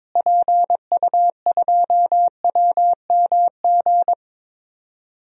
Morse code, heard as a keyed tone, about 22 wpm, 700 Hz. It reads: PU2WMG
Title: DXCC-100 Call Signs - 6 Characters - Repeated 2x - 22wpm